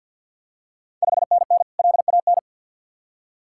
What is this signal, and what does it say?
Morse code, heard as a keyed tone, about 46 wpm, 700 Hz. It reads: 5NN LAN
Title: Ontario QSO Party Exchange 50wpm